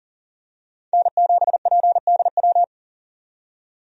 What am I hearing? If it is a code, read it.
Morse code, heard as a keyed tone, about 40 wpm, 700 Hz. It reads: N7PDW
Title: USA Call Signs - 5 Characters - 40wpm